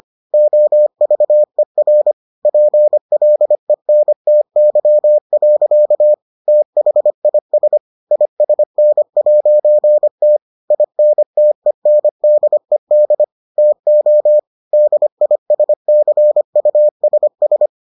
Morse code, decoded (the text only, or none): OVER PLENTY. THIS ISN'T INTENDED TO DISCUSS